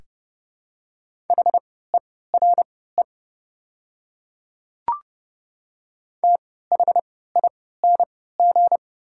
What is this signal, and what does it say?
Morse code, heard as a keyed tone, about 30 wpm, spaced as for 17 wpm, 700 Hz. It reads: HERE THING
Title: 3x Character Spacing - Top 300 Words - 30wpm